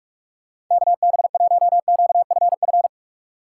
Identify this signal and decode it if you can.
Morse code, heard as a keyed tone, about 46 wpm, 700 Hz. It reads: KB1QFF
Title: USA Call Signs - 6 Characters - 4x Character Spacing - 45wpm